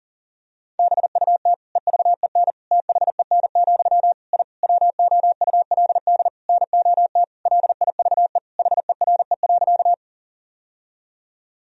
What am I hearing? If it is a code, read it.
Morse code, heard as a keyed tone, about 40 wpm, 700 Hz. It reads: BUT EVEN THEN, I WOULD NOT LIVE HERE.
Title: Sentences from Top 1000 Words - Review Part 1 - 40wpm